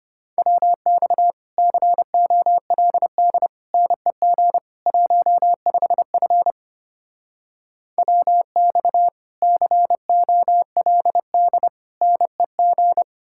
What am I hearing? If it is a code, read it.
Morse code, heard as a keyed tone, about 33 wpm, 700 Hz. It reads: WX COLD NEG 15F WX COLD NEG